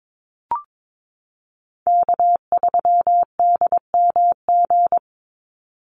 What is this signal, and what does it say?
Morse code, heard as a keyed tone, about 22 wpm, 700 Hz. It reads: K3DMG
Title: Call Signs Repeated 2x - 22wpm